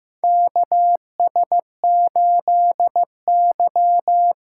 Morse code, heard as a keyed tone, about 15 wpm, 700 Hz. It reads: KS8Y
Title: USA Call Signs - 4 Characters - 4x Character Spacing - 15wpm